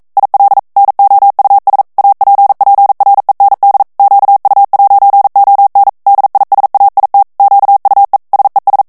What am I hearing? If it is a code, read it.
Morse code, heard as a keyed tone, about 42 wpm, 800 Hz. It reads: IL NOUS APPREND QU'ON DISAIT QUE SES
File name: CW / FICHIERS-MP3 / FRANCAIS TEXTE CLAIR 2 VITESSE 40 MOTS MINUTES